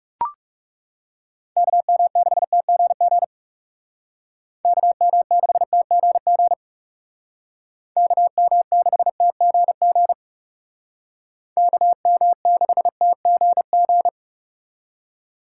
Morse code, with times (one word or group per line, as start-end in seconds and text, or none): 1.6-3.2 KM6TGG
4.7-6.5 KM6TGG
8.0-10.1 KM6TGG
11.6-14.1 KM6TGG